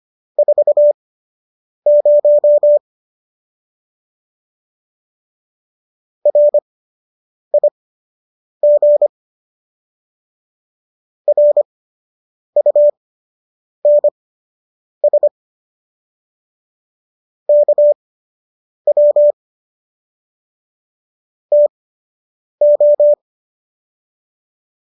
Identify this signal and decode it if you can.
Morse code, heard as a keyed tone, about 25 wpm, spaced as for 8 wpm, 590 Hz. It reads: 40 RIG RUNS KW TO